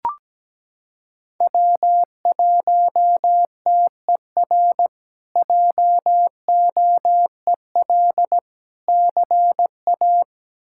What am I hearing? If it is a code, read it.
Morse code, heard as a keyed tone, about 17 wpm, 700 Hz. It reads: W1TER JOEL CA
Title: K1USN SST QSO Practice - 17wpm